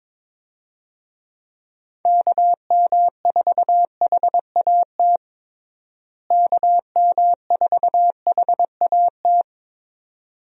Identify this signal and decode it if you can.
Morse code, heard as a keyed tone, about 22 wpm, 700 Hz. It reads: KM4HAT KM4HAT
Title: USA Call Signs - 6 Characters - Repeated 2x - 22wpm